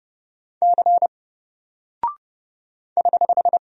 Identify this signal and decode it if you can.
Morse code, heard as a keyed tone, about 30 wpm, 700 Hz. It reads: C <HH>
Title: Single Character (Letter:Number:Punctuation:Prosign) 30wpm